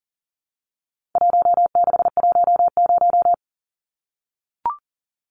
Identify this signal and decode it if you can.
Morse code, heard as a keyed tone, about 39 wpm, 700 Hz. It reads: 1610
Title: Numbers - 4-Digit 40wpm